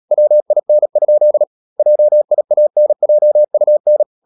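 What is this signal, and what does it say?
Morse code, heard as a keyed tone, about 37 wpm, 600 Hz. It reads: WIN? JIANJUN
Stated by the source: Morse code practice at 35–39 wpm sending English text of Ken Follett's novel Never, from CW med Gnister.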